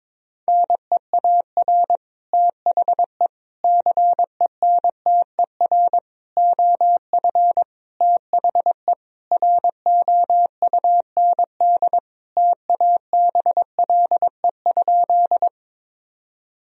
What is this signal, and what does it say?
Morse code, heard as a keyed tone, about 22 wpm, 700 Hz. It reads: NEAR THE CENTER OF THE ROUND TABLE?
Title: Sentences from Top 1000 Words - Review Part 2 - 22wpm